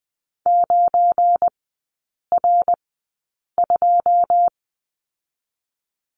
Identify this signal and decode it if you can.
Morse code, heard as a keyed tone, about 20 wpm, spaced as for 8 wpm, 690 Hz. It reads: 9R2
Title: 191 Lesson 31e - Letter v - Three Characters 20wpm